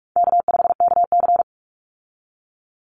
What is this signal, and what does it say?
Morse code, heard as a keyed tone, about 46 wpm, 700 Hz. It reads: K5KC
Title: Easy Call Signs - 45wpm